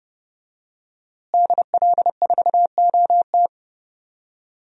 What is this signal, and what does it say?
Morse code, heard as a keyed tone, about 30 wpm, 700 Hz. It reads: DL4OT
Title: USA Call Signs - 5 Characters - Repeated 3x - 30wpm